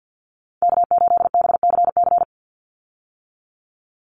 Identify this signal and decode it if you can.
Morse code, heard as a keyed tone, about 49 wpm, 700 Hz. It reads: K8BCC
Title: USA Call Signs 50wpm